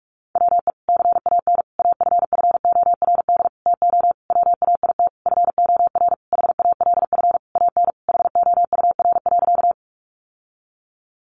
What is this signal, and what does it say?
Morse code, heard as a keyed tone, about 46 wpm, 700 Hz. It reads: WE CAN AFFORD TO WAIT FOR HALF AN HOUR.
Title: Sentences from Top 1100 Words 45wpm